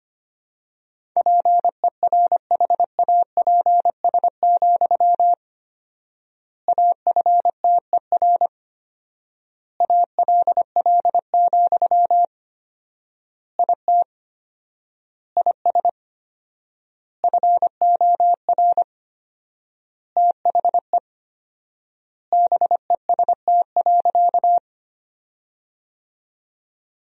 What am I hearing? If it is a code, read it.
Morse code, heard as a keyed tone, about 25 wpm, 700 Hz. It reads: PERHAPS, AFTER ALL, IT IS FOR THE BEST.
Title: 4x Word-Spacing - Sentences from Top 500 Words - 25wpm